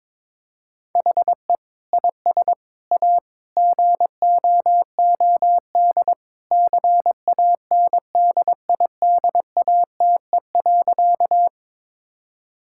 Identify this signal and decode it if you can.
Morse code, heard as a keyed tone, about 21 wpm, 700 Hz. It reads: HE IS A GOOD CANDIDATE.
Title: Sentences from Top 1600 Words 22wpm